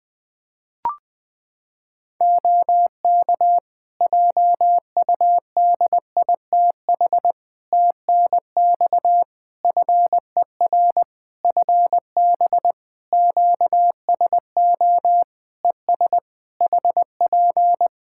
Morse code, heard as a keyed tone, about 21 wpm, 700 Hz. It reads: OK JUDITH TNX FER FB QSO ES HP
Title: Practice QSO - Exchange 4 - Ending - 20wpm